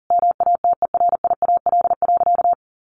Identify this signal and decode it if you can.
Morse code, heard as a keyed tone, about 40 wpm, 700 Hz. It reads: MATERIAL.
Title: Sentences from Top 1100 Words 40wpm